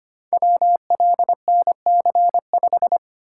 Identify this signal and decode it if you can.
Morse code, heard as a keyed tone, about 25 wpm, 700 Hz. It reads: WLNC5
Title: Code Groups - 5 Letter:Numbers - 25wpm